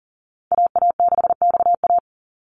Morse code, heard as a keyed tone, about 37 wpm, 700 Hz. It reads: AA6XA
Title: NA SOTA Activator Callsigns 40wpm